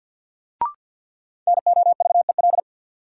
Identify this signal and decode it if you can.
Morse code, heard as a keyed tone, about 52 wpm, 700 Hz. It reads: NOVEL
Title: Sentences from Top 1300 Words - New Word Drill Alt - 50wpm